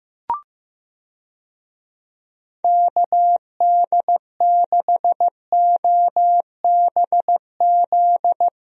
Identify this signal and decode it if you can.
Morse code, heard as a keyed tone, about 15 wpm, 700 Hz. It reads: KD6OBZ
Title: USA Call Signs - 6 Characters - 15wpm